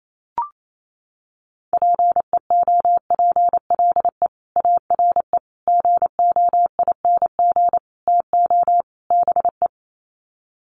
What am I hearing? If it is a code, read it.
Morse code, heard as a keyed tone, about 28 wpm, 700 Hz. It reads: PEOPLE ARE GOING TO BE
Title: Sets of 5 Words Encore 28wpm